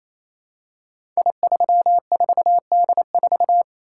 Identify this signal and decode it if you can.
Morse code, heard as a keyed tone, about 28 wpm, 700 Hz. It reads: I34D4
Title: Code Groups - 5 Letter:Numbers - 28wpm